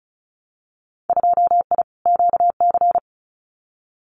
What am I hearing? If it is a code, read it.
Morse code, heard as a keyed tone, about 36 wpm, 700 Hz. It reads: 2I QC
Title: Winter Field Day Exchanges 35wpm